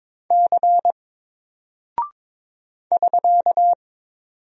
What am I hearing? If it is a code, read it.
Morse code, heard as a keyed tone, about 22 wpm, 700 Hz. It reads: C <SK>
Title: Single Character (Letter:Number:Punctuation:Prosign) 22wpm